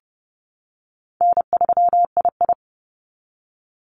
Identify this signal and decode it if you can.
Morse code, heard as a keyed tone, about 29 wpm, 700 Hz. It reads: N3II
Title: NA SOTA Activator Callsigns 30wpm